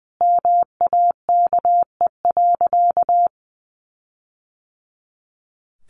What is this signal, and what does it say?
Morse code, heard as a keyed tone, about 20 wpm, 700 Hz. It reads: MAKE.
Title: Sentences from Top 1400 Words 20wpm